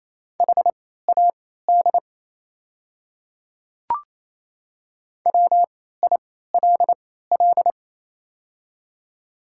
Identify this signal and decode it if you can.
Morse code, heard as a keyed tone, about 28 wpm, spaced as for 16 wpm, 700 Hz. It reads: HAD WILL
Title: 3x Character Spacing - Top 100 Words - 28wpm